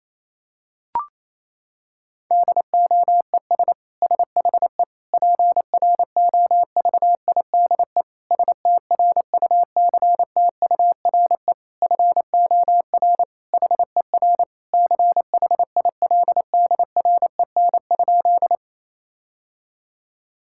Morse code, heard as a keyed tone, about 28 wpm, 700 Hz. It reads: DOES SHE PROVIDE STRUCTURE FOR HER CHILDREN?
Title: Sentences from Top 1000 Words - Review Part 6 - 28wpm